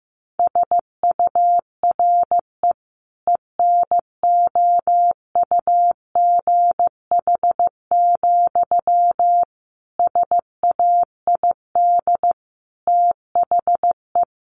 Morse code, heard as a keyed tone, about 15 wpm, 700 Hz. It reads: SURE ENOUGH, SAID THE
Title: Sentences from Top 600 Words 15wpm